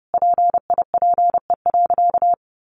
Morse code, heard as a keyed tone, about 30 wpm, 700 Hz. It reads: PIPE.
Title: Sentences from Top 1500 Words 30wpm